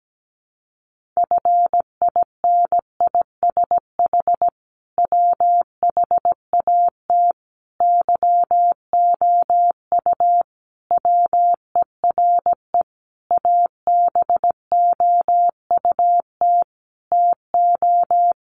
Morse code, heard as a keyed tone, about 17 wpm, 700 Hz. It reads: FINISH WHAT YOU WERE ABOUT TO
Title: Sentences from Top 1200 Words 17wpm